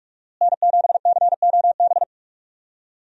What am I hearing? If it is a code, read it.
Morse code, heard as a keyed tone, about 46 wpm, 700 Hz. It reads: N7COB
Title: USA Call Signs - 5 Characters - Repeated 3x - 45wpm